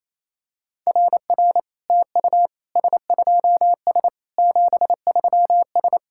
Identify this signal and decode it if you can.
Morse code, heard as a keyed tone, about 28 wpm, 700 Hz. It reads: RR TU S2S 73S
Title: SOTA QSO Practice 28wpm